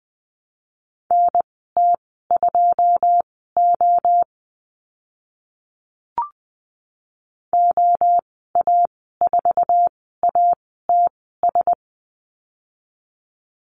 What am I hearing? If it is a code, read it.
Morse code, heard as a keyed tone, about 20 wpm, spaced as for 14 wpm, 700 Hz. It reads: NT2O OA4ATS
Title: 082 Lesson 14g - Number 2 - Call Signs 20wpm